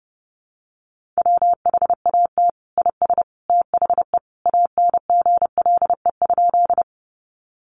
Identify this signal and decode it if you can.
Morse code, heard as a keyed tone, about 30 wpm, 700 Hz. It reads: WHAT IS THE ANGLE?
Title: Sentences from Top 1500 Words 30wpm